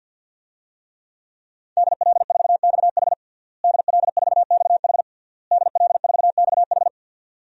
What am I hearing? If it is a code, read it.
Morse code, heard as a keyed tone, about 50 wpm, 700 Hz. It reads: DL4XH DL4XH DL4XH
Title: DXCC-100 Call Signs - 5 Characters - Repeated 3x - 50wpm